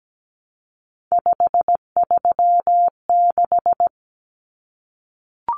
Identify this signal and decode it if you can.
Morse code, heard as a keyed tone, about 17 wpm, 700 Hz. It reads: 536
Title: Common Cut Numbers - 3-Digit 17wpm